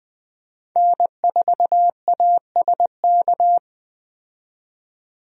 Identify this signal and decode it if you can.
Morse code, heard as a keyed tone, about 20 wpm, 700 Hz. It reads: N4ASK
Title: USA Call Signs - 5 Characters - 20wpm